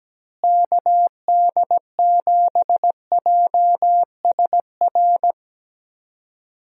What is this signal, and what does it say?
Morse code, heard as a keyed tone, about 17 wpm, 700 Hz. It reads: KD7JSR
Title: USA Call Signs - 6 Characters - Repeated 3x - 17wpm